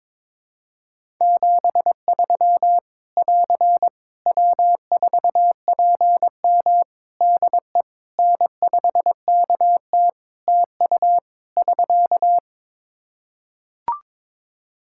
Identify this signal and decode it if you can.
Morse code, heard as a keyed tone, about 22 wpm, 700 Hz. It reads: 73 <AR> W4PM DE N5KT TU <SK>
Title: K1USN SST QSO Practice 22wpm